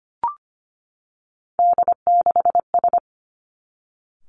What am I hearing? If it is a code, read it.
Morse code, heard as a keyed tone, about 25 wpm, 700 Hz. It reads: D6S
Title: Three Challenging Letter-Number Combinations 25wpm